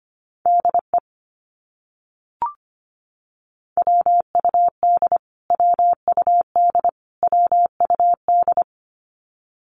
Morse code, heard as a keyed tone, about 25 wpm, 700 Hz. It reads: DE WUD WUD WUD
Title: Abbreviations Repeated 3x - 25wpm